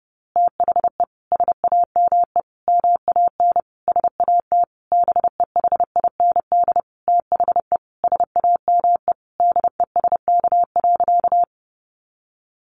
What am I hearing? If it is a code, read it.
Morse code, heard as a keyed tone, about 30 wpm, 700 Hz. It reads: THE SAME MAN SAT BEHIND THE SAME DESK.
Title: A Sound Of Thunder 30wpm